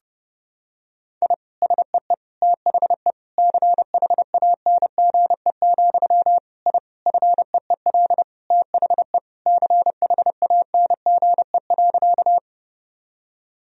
Morse code, heard as a keyed tone, about 30 wpm, 700 Hz. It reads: I SEE THE CHANGE, I FEEL THE CHANGE.
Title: Sentences from Top 1000 Words - Review Part 1 - 30wpm